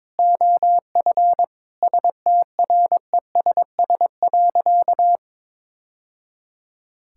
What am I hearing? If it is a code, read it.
Morse code, heard as a keyed tone, about 22 wpm, 700 Hz. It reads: OF STRESS.
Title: Sentences from Top 1000 Words - Review Part 6 - 22wpm